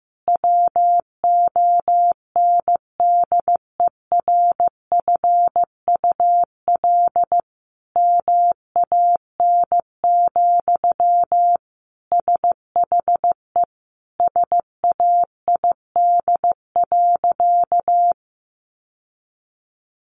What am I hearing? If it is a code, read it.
Morse code, heard as a keyed tone, about 15 wpm, 700 Hz. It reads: WONDERFUL MAN, SHE SAID.
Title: Sentences from Top 1100 Words 15wpm